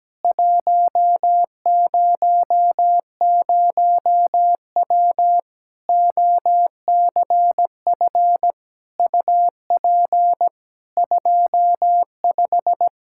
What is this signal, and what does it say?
Morse code, heard as a keyed tone, about 17 wpm, 700 Hz. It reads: 100W OCF UP 25
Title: Practice QSO - Exchange 2 - Info - 17wpm